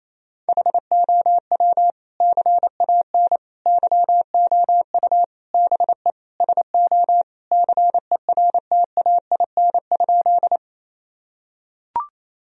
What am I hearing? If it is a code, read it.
Morse code, heard as a keyed tone, about 27 wpm, 700 Hz. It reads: HOW CAN YOU BE SO CERTAIN?
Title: Sentences from Top 1000 Words - Review Part 2 - 28wpm